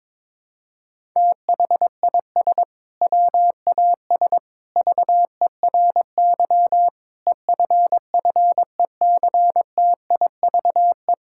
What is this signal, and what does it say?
Morse code, heard as a keyed tone, about 22 wpm, 700 Hz. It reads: THIS WAS VERY EFFECTIVE
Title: Sentences from Top 1000 Words - Review Part 6 - 22wpm